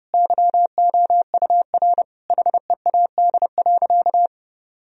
Morse code, heard as a keyed tone, about 32 wpm, 700 Hz. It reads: YOUR HEAD.
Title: Sentences from Top 1000 Words - Review Part 1 - 30wpm